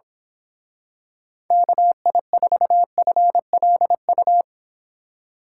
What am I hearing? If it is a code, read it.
Morse code, heard as a keyed tone, about 27 wpm, 700 Hz. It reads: KI4FLU
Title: Speed Racing - USA Call Signs 22wpm